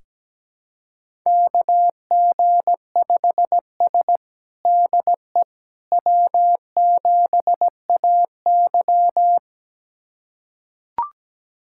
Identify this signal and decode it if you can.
Morse code, heard as a keyed tone, about 17 wpm, 700 Hz. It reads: KG5S DE W7AY
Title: USA Callsign DE Callsign - 4 Characters - 17wpm